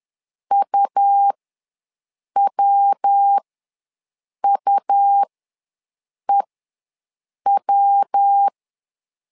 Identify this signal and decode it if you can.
Morse code, heard as a keyed tone, about 10 wpm, spaced as for 6 wpm, 800 Hz. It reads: UWUEW
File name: LEZIONI / 3 MWRO / 3 MW ws 10wpm 2213